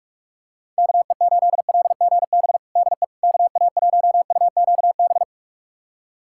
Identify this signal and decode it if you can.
Morse code, heard as a keyed tone, about 41 wpm, 700 Hz. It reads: KE8LGB DE KA1UQB
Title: USA Callsign DE Callsign - 6 Characters - 45wpm